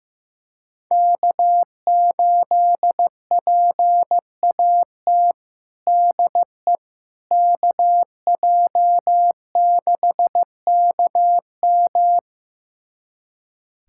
Morse code, heard as a keyed tone, about 15 wpm, 700 Hz. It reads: K8PAT DE KJ6KM
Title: USA Callsign DE Callsign - 5 Characters - 15wpm